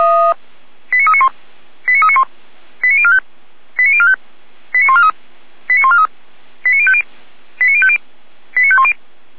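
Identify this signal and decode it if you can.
Five-tone selective call (ZVEI), heard as quick runs of five tones, tones of about 70 ms. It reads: 89391 89391 89045 89045 89143 89143 89050 89050 89510
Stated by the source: ZVEI five-tone call received by SDR, server of feuerwehr-orlamuende (fire brigade Orlamünde).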